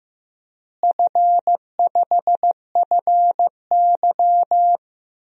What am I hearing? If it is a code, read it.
Morse code, heard as a keyed tone, about 15 wpm, 700 Hz. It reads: F5FY
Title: DXCC-100 Call Signs - 4 Characters - Repeated 3x - 15wpm